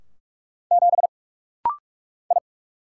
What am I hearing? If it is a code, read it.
Morse code, heard as a keyed tone, about 44 wpm, 700 Hz. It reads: 7 I
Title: Single Letter-Number - Rapid-Fire 45wpm